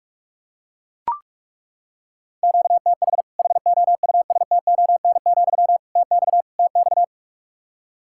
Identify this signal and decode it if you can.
Morse code, heard as a keyed tone, about 41 wpm, 700 Hz. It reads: QTH HOUSTON, TX TX
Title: Practice QSO - Exchange 1 - Beginning - 45wpm